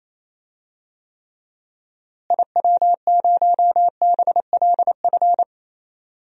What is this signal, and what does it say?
Morse code, heard as a keyed tone, about 27 wpm, 700 Hz. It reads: IW0BLF
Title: DXCC-100 Call Signs - 6 Characters - Repeated 3x - 28wpm